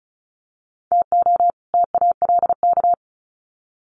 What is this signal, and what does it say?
Morse code, heard as a keyed tone, about 34 wpm, 700 Hz. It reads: TO TALK
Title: Sets of 2 Words 35wpm